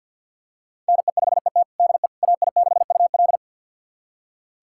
Morse code, heard as a keyed tone, about 47 wpm, 700 Hz. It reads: NE5ET DE AI6UL
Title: USA Callsign DE Callsign - 5 Characters - 50wpm